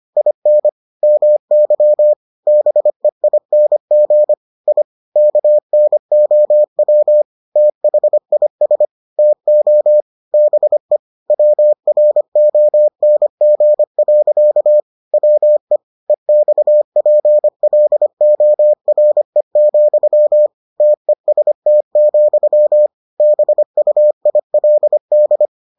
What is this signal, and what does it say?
Morse code, heard as a keyed tone, about 25 wpm, 600 Hz. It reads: IN MY BEING I KNOW THIS TO BE WRONG. WE EXPLORE, TEST, BUILD